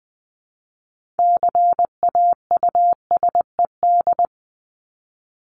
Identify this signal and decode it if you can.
Morse code, heard as a keyed tone, about 20 wpm, 700 Hz. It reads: CAUSED
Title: ED Suffix - Top 100 Words with Base Words 20wpm